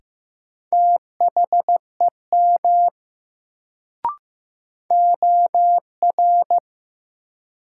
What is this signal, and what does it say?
Morse code, heard as a keyed tone, about 15 wpm, 700 Hz. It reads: THEM OR
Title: Sets of 2 Words - New Word Drill - 15wpm